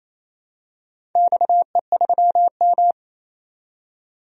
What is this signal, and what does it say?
Morse code, heard as a keyed tone, about 29 wpm, 700 Hz. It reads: XE3M
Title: DXCC-100 Call Signs - 4 Characters - 28wpm